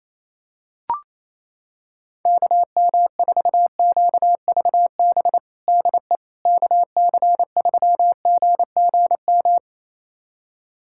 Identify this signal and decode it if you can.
Morse code, heard as a keyed tone, about 28 wpm, 700 Hz. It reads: KM4QVB DE KC3GGM
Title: USA Callsign DE Callsign - 6 Characters - 28wpm